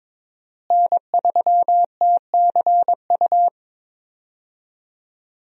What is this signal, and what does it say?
Morse code, heard as a keyed tone, about 21 wpm, 700 Hz. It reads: N3TCU
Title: USA Call Signs - 5 Characters - Repeated 2x - 22wpm